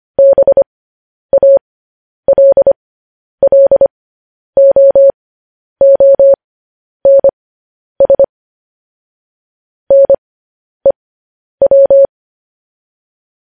Morse code, heard as a keyed tone, about 25 wpm, spaced as for 10 wpm, 550 Hz. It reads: BALLOONS NEW